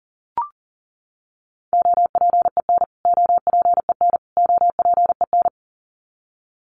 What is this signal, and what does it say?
Morse code, heard as a keyed tone, about 40 wpm, 700 Hz. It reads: OPEN OPEN OPEN
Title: Top 500 Words Repeated 40wpm